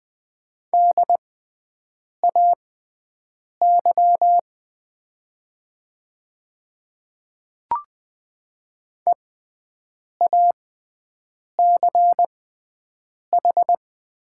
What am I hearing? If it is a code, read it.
Morse code, heard as a keyed tone, about 20 wpm, spaced as for 7 wpm, 700 Hz. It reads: DAY EACH
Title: 6x Character Spacing - Top 100 Words - 20wpm